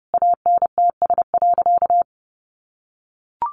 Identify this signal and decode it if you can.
Morse code, heard as a keyed tone, about 30 wpm, 700 Hz. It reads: ANTS.
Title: Easy Phrases 30wpm